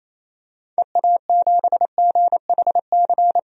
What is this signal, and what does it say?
Morse code, heard as a keyed tone, about 28 wpm, 700 Hz. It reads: EA7GHC
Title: DXCC-100 Call Signs - 6 Characters - 28wpm